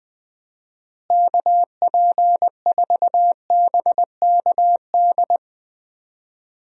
Morse code, heard as a keyed tone, about 20 wpm, 700 Hz. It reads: KP4BKD
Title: DXCC-100 Call Signs - 6 Characters - Repeated 3x - 20wpm